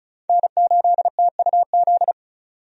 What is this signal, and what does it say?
Morse code, heard as a keyed tone, about 34 wpm, 700 Hz. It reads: N8TUZ
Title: USA Call Signs - 5 Characters - 35wpm